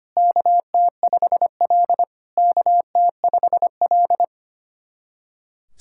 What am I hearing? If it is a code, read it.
Morse code, heard as a keyed tone, about 25 wpm, 690 Hz. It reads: KT5L KT5L
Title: SOTA QSO Practice 25wpm